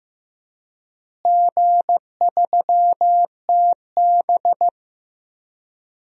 Morse code, heard as a keyed tone, about 15 wpm, 700 Hz. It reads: G3TB
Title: DXCC-100 Call Signs - 4 Characters - Repeated 2x - 15wpm